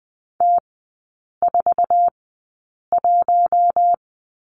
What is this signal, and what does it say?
Morse code, heard as a keyed tone, about 20 wpm, spaced as for 8 wpm, 700 Hz. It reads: T41
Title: 067 Lesson 12e - Letter d - Three Characters 20wpm